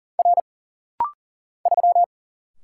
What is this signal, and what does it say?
Morse code, heard as a keyed tone, about 40 wpm, 700 Hz. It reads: R 3
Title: Single Letter-Number - Rapid-Fire 40wpm